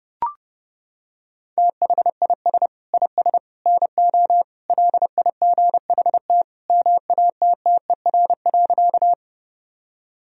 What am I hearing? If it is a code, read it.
Morse code, heard as a keyed tone, about 31 wpm, 700 Hz. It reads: THIS IS NO LIGHT MATTER.
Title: Sentences from Top 1000 Words - Review Part 1 - 30wpm